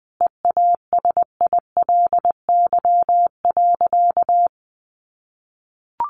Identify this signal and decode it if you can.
Morse code, heard as a keyed tone, about 20 wpm, 700 Hz. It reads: EASILY.
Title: Sentences from Top 900 Words 20wpm